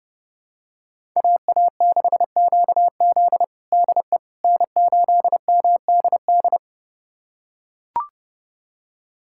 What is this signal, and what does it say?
Morse code, heard as a keyed tone, about 30 wpm, 700 Hz. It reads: AA6QZ DE N8MDD
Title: USA Callsign DE Callsign - 5 Characters - 30wpm